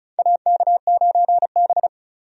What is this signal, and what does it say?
Morse code, heard as a keyed tone, about 36 wpm, 700 Hz. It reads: AK9B
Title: USA Call Signs - 4 Characters - 35wpm